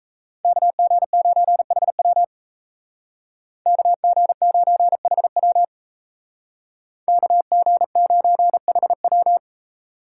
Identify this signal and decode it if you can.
Morse code, heard as a keyed tone, about 36 wpm, 700 Hz. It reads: KG9HW KG9HW KG9HW